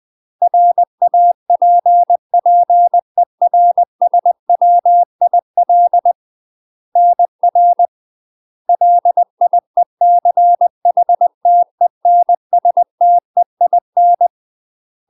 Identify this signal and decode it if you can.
Morse code, heard as a keyed tone, about 20 wpm, 700 Hz. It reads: RAPPERSWIL NR LIECHTENSTEIN
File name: UK MASTER QSOs TRAINING / 20WPM / UK MASTER QSOs 0018